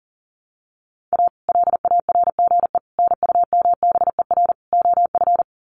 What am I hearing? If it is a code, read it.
Morse code, heard as a keyed tone, about 40 wpm, 700 Hz. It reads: A LARGE NUMBER OF
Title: Sets of 4 Words 40wpm